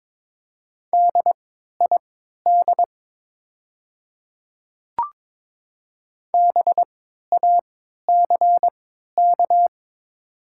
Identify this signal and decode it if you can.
Morse code, heard as a keyed tone, about 22 wpm, spaced as for 12 wpm, 700 Hz. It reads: DID BACK
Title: 3x Character Spacing - Top 200 Words - 22wpm